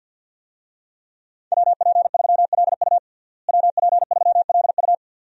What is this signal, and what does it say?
Morse code, heard as a keyed tone, about 53 wpm, 700 Hz. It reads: WP3LU WP3LU
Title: USA Call Signs - 5 Characters - Repeated 2x - 50wpm